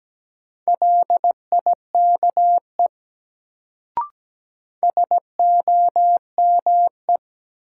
Morse code, heard as a keyed tone, about 17 wpm, 700 Hz. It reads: LIKE SOME
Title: Sentences from Top 100 Words - New Word Drill - 17wpm